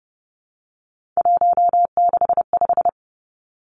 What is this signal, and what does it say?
Morse code, heard as a keyed tone, about 30 wpm, 700 Hz. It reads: 165
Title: Three Challenging Letter-Number Combinations 30wpm